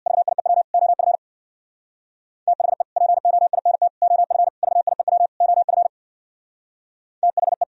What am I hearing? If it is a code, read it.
Morse code, heard as a keyed tone, about 64 wpm, 700 Hz. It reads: VIEW OF THE POINT OF VIEW OF THE